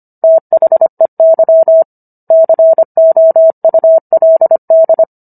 Morse code, heard as a keyed tone, about 25 wpm, 650 Hz. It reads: THEY COULD